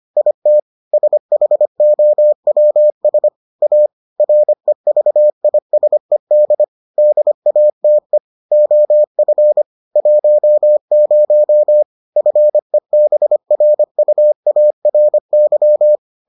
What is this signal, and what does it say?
Morse code, heard as a keyed tone, about 25 wpm, 600 Hz. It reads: IT SHOWS A REVISED DATE OF 10 FEBRUARY